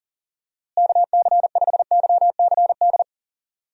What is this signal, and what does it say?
Morse code, heard as a keyed tone, about 40 wpm, 700 Hz. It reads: KC5YCD
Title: USA Call Signs - 6 Characters - Repeated 2x - 40wpm